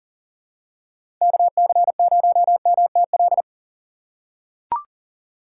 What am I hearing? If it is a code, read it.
Morse code, heard as a keyed tone, about 40 wpm, 700 Hz. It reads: KC0MTL
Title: USA Call Signs - 6 Characters - 3x Character Spacing - 40wpm